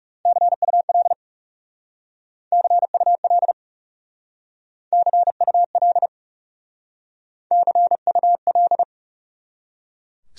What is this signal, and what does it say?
Morse code, heard as a keyed tone, about 38 wpm, 700 Hz. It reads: CUL CUL CUL CUL